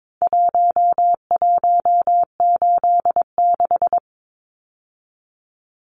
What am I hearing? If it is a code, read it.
Morse code, heard as a keyed tone, about 22 wpm, 700 Hz. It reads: 1186
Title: Numbers - Contest Style - 4-Digit 22wpm